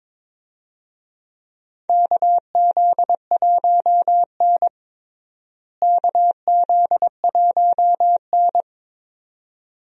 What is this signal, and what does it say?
Morse code, heard as a keyed tone, about 22 wpm, 700 Hz. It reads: KZ1N KZ1N
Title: USA Call Signs - 4 Characters - Repeated 2x - 22wpm